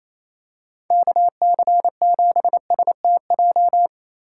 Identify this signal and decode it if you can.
Morse code, heard as a keyed tone, about 28 wpm, 700 Hz. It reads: KC7STJ
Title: USA Call Signs - 6 Characters - 4x Character Spacing - 28wpm